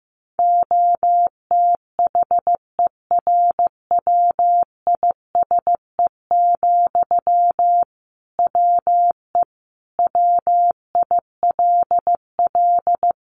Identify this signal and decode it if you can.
Morse code, heard as a keyed tone, about 15 wpm, 700 Hz. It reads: OTHERWISE, WE WILL
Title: Sentences from Top 1500 Words 15wpm